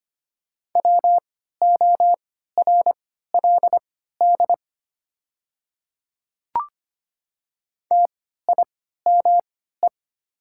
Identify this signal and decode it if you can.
Morse code, heard as a keyed tone, about 25 wpm, spaced as for 14 wpm, 700 Hz. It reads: WORLD TIME
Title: 3x Character Spacing - Top 200 Words - 25wpm